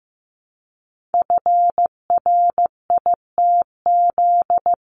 Morse code, heard as a keyed tone, about 15 wpm, 700 Hz. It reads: FRITZ
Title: CWT Exchange 15wpm